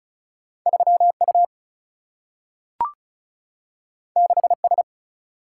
Morse code, heard as a keyed tone, about 36 wpm, 700 Hz. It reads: 3 U 6 S
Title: Single-Letter Single-Number 35wpm